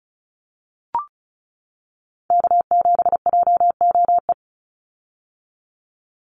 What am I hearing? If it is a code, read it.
Morse code, heard as a keyed tone, about 34 wpm, 700 Hz. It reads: K7JOE
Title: Call Signs - 35wpm